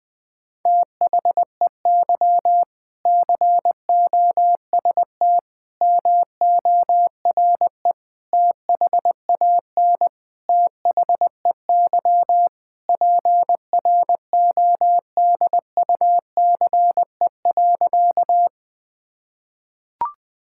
Morse code, heard as a keyed tone, about 20 wpm, 700 Hz. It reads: THEY COST MORE THAN THEY PRODUCE.
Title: Sentences from Top 1000 Words - Review Part 5 - 20wpm